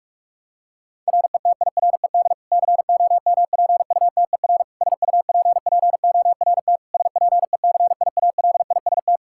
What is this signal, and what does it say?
Morse code, heard as a keyed tone, about 46 wpm, 700 Hz. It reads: RETIRED COMPUTER SUPPORT SPECIALIST
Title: Practice QSO - Exchange 3 - More Info - 45wpm